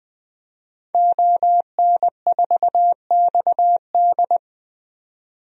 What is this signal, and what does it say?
Morse code, heard as a keyed tone, about 21 wpm, 700 Hz. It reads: ON4XD
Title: DXCC-100 Call Signs - 5 Characters - 20wpm